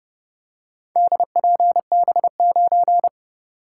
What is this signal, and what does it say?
Morse code, heard as a keyed tone, about 27 wpm, 700 Hz. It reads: DPB9